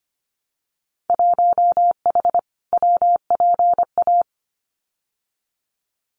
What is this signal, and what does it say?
Morse code, heard as a keyed tone, about 25 wpm, 700 Hz. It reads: 1H WPA
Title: Winter Field Day Exchanges 25wpm